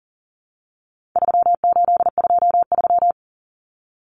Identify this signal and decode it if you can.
Morse code, heard as a keyed tone, about 40 wpm, 700 Hz. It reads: 3823
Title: Common Cut Numbers - 4-Digit 40wpm